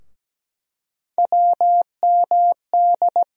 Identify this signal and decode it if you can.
Morse code, heard as a keyed tone, about 17 wpm, 700 Hz. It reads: WMD
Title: USA Call Sign Postfixes - 3 Letters 17wpm